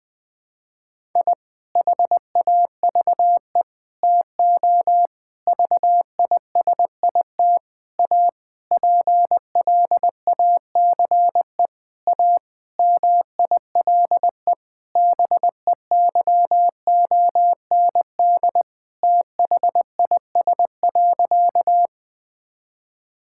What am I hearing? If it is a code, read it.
Morse code, heard as a keyed tone, about 20 wpm, 700 Hz. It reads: I HAVE TO VISIT A PLACE A MILE BEYOND THIS.
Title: Sentences from Top 1000 Words - Review Part 6 - 20wpm